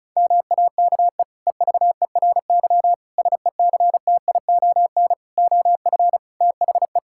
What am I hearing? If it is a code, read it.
Morse code, heard as a keyed tone, about 36 wpm, 700 Hz. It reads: MAKE EVERY SECTION OF THE
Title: Sentences from Top 1000 Words - Review Part 5 - 35wpm